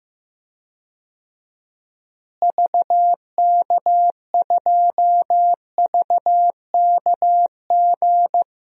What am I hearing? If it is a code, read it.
Morse code, heard as a keyed tone, about 15 wpm, 700 Hz. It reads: VK2VKG
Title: DXCC-100 Call Signs - 6 Characters - Repeated 2x - 15wpm